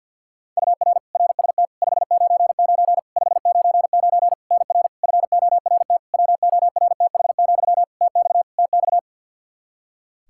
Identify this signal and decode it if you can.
Morse code, heard as a keyed tone, about 51 wpm, 700 Hz. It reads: UR RST 599 599 NR FORT WORTH, TX TX
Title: Practice QSO - Exchange 1 - Beginning - 50wpm